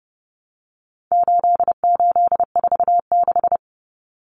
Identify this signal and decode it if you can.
Morse code, heard as a keyed tone, about 30 wpm, 700 Hz. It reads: 8846
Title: Numbers - 4-Digit 30wpm